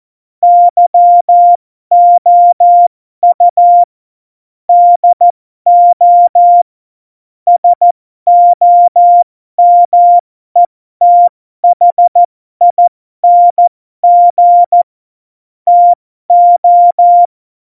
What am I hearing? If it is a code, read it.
Morse code, heard as a keyed tone, about 14 wpm, spaced as for 12 wpm, 700 Hz. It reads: YOU DO SOMETHING TO
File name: Morsekurs / Witze